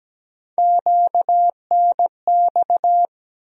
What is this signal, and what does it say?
Morse code, heard as a keyed tone, about 17 wpm, 700 Hz. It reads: QNX
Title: USA Call Sign Postfixes - 3 Letters 17wpm